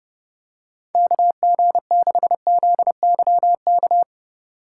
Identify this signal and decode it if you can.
Morse code, heard as a keyed tone, about 30 wpm, 700 Hz. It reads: KG6ZYK
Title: USA Call Signs - 6 Characters - 2x Character Spacing - 30wpm